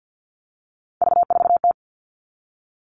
Morse code, heard as a keyed tone, about 48 wpm, 700 Hz. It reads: V4T
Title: All Cut Numbers - 3-Digit 50wpm